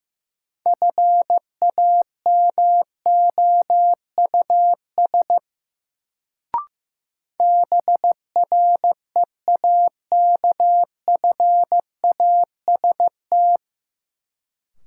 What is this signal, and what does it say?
Morse code, heard as a keyed tone, about 15 wpm, 700 Hz. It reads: FAMOUS BREAKFAST
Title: Sentences from Top 1500 Words - New Word Drill - 15wpm